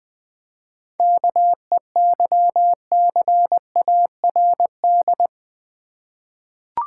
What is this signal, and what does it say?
Morse code, heard as a keyed tone, about 20 wpm, 690 Hz. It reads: KEYCARD
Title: Easy Words with Hints - 20wpm